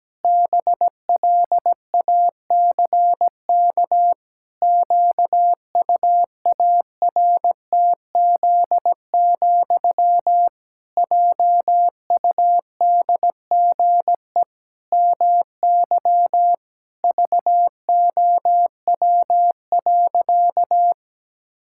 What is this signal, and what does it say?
Morse code, heard as a keyed tone, about 17 wpm, 700 Hz. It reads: BLACK QUARTZ, JUDGE MY VOW.
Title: Pangram Challenge v2 - 17wpm